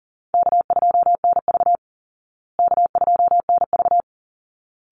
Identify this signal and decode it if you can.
Morse code, heard as a keyed tone, about 40 wpm, 700 Hz. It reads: K2NV K2NV
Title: Call Signs Repeated 3x - 40wpm